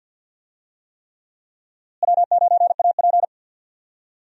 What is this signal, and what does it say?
Morse code, heard as a keyed tone, about 52 wpm, 700 Hz. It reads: W9AP
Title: USA Call Signs - 4 Characters - Repeated 3x - 50wpm